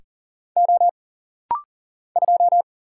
Morse code, heard as a keyed tone, about 41 wpm, 700 Hz. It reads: O 2
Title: Single Letter-Number Alt - Rapid-Fire 40wpm